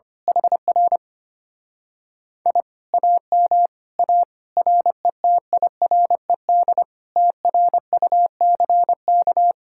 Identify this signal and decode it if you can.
Morse code, heard as a keyed tone, about 25 wpm, 700 Hz. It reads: HR I AM A RETIRED TRUCK
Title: Speed-Racing Common QSO Phrases Encore - 25wpm